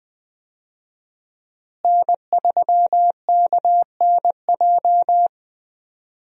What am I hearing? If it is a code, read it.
Morse code, heard as a keyed tone, about 21 wpm, 700 Hz. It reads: N3KNJ
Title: USA Call Signs - 5 Characters - Repeated 2x - 20wpm